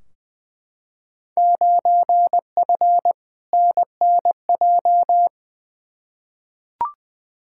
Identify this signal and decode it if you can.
Morse code, heard as a keyed tone, about 20 wpm, 700 Hz. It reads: 9F NNJ
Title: Field Day Exchanges Encore - 20wpm